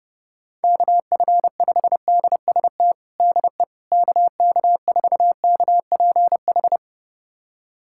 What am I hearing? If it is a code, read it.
Morse code, heard as a keyed tone, about 30 wpm, 690 Hz. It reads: KF5DST DE KK4KPH
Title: USA Callsign DE Callsign 30wpm